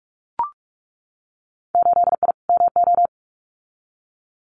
Morse code, heard as a keyed tone, about 44 wpm, 700 Hz. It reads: 8I MO
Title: Winter Field Day Exchanges Encore 45wpm